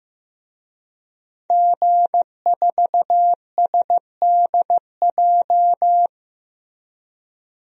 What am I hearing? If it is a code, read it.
Morse code, heard as a keyed tone, about 15 wpm, 700 Hz. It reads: G4SDJ
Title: DXCC-100 Call Signs - 5 Characters - Repeated 2x - 15wpm